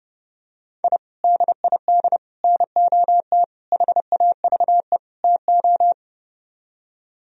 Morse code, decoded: I DID NOT HAVE TO

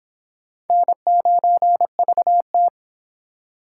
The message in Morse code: N9VT

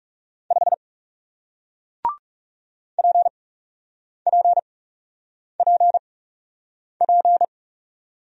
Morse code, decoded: 5 PPPP

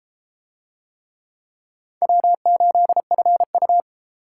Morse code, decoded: W8FU

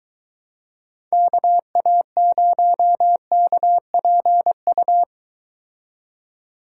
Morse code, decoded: KA0KPU

